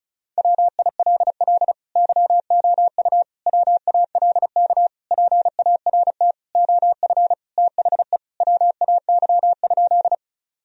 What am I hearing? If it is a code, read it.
Morse code, heard as a keyed tone, about 36 wpm, 700 Hz. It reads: WILL YOU WALK PART OF THE WAY?
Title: Sentences from Top 1000 Words - Review Part 1 - 35wpm